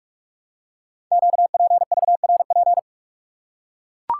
Morse code, heard as a keyed tone, about 44 wpm, 700 Hz. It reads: QPVRP
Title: Code Groups - 5 Letters - 45wpm